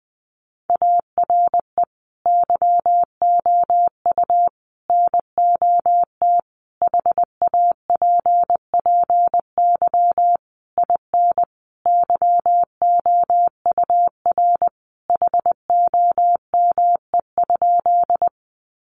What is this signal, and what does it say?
Morse code, heard as a keyed tone, about 20 wpm, 700 Hz. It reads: ARE YOU NOT HAPPY IN YOUR HOME?
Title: Sentences from Top 1000 Words 20wpm